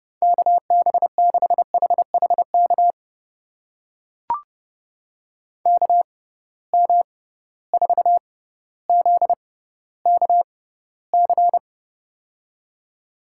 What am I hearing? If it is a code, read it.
Morse code, heard as a keyed tone, about 32 wpm, spaced as for 16 wpm, 700 Hz. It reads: KB6HHK KM4ZKC
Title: USA Call Signs - 6 Characters - 6x Character Spacing - 30wpm